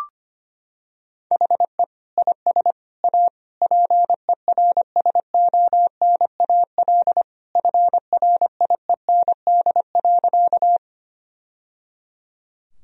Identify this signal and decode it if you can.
Morse code, heard as a keyed tone, about 25 wpm, 700 Hz. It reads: HE IS A PERSONAL FRIEND.
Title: Sentences from Top 1000 Words - Review Part 4 - 25wpm